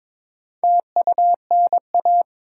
Morse code, heard as a keyed tone, about 22 wpm, 700 Hz. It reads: TUNA
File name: Difficult Words with Hints - 22wpm